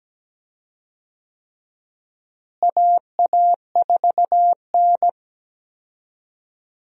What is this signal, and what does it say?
Morse code, heard as a keyed tone, about 17 wpm, 700 Hz. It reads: AA4N
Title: USA Call Signs - 4 Characters - Repeated 3x - 17wpm